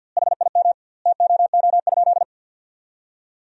Morse code, heard as a keyed tone, about 45 wpm, 700 Hz. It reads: HIM TOO?
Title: 2x Word-Spacing - Sentences from Top 500 Words - 50wpm